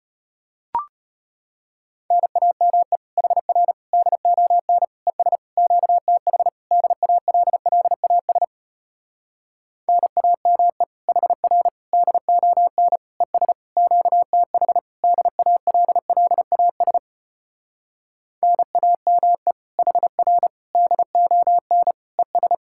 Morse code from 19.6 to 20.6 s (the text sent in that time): HR